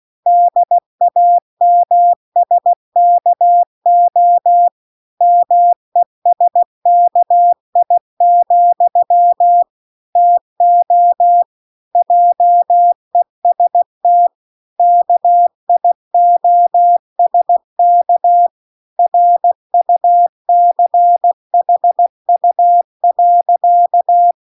Morse code, heard as a keyed tone, about 16 wpm, 700 Hz. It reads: DAMSKO MESKI, TO JEST KIOSK RUCHU.